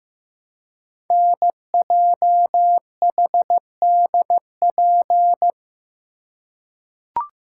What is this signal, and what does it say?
Morse code, heard as a keyed tone, about 15 wpm, 700 Hz. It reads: NJHDP
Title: Code Groups - 5 Letters - 15wpm